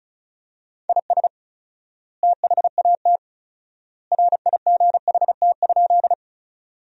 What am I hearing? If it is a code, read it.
Morse code, heard as a keyed tone, about 34 wpm, 700 Hz. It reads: IS THAT RIGHT?
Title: 4x Word-Spacing - Sentences from Top 200 Words - 35wpm